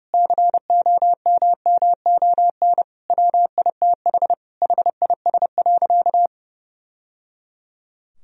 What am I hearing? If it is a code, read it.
Morse code, heard as a keyed tone, about 30 wpm, 700 Hz. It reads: COMMON WITH HIS.
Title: Sentences from Top 1000 Words - Review Part 3 - 30wpm